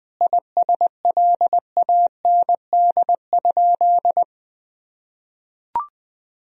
Morse code, heard as a keyed tone, about 21 wpm, 700 Hz. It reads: ISLAND?
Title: Sentences from Top 1000 Words - Review Part 3 - 20wpm